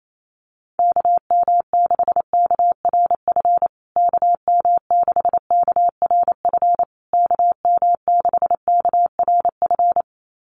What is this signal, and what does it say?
Morse code, heard as a keyed tone, about 28 wpm, 700 Hz. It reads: KM6KRF KM6KRF KM6KRF
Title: USA Call Signs Repeated 3x 28wpm